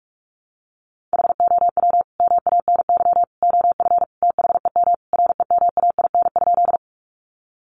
Morse code, heard as a keyed tone, about 44 wpm, 700 Hz. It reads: HOW MANY OF THEM REMAIN?
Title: Sentences from Top 1200 Words 45wpm